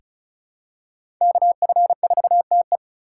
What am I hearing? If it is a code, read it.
Morse code, heard as a keyed tone, about 36 wpm, 700 Hz. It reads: KF4TE
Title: USA Call Signs - 5 Characters - 4x Character Spacing - 35wpm